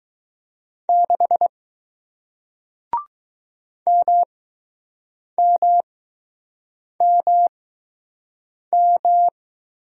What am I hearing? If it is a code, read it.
Morse code, heard as a keyed tone, about 20 wpm, spaced as for 6 wpm, 700 Hz. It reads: 6 MMMM